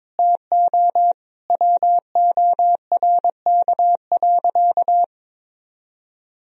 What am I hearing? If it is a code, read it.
Morse code, heard as a keyed tone, about 22 wpm, 700 Hz. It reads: TO WORK.
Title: Sentences from Top 1000 Words - Review Part 1 - 22wpm